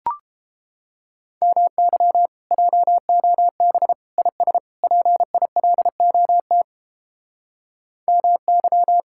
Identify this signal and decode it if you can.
Morse code, heard as a keyed tone, about 31 wpm, 700 Hz. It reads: MY JOB IS PILOT MY